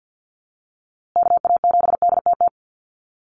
Morse code, heard as a keyed tone, about 50 wpm, 700 Hz. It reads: KA7DTT
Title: USA Call Signs Repeated 2x 50wpm